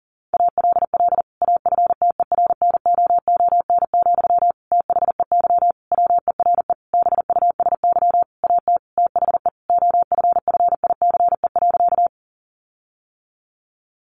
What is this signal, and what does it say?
Morse code, heard as a keyed tone, about 40 wpm, 700 Hz. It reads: ALL AFTERNOON, THEY WERE BUSY AT THE OFFICE.
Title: Sentences from Top 1500 Words 40wpm